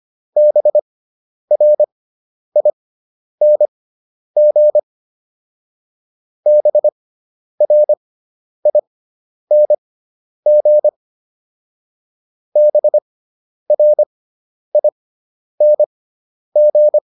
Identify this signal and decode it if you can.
Morse code, heard as a keyed tone, about 25 wpm, spaced as for 10 wpm, 600 Hz. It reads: BRING BRING BRING
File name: ING5 10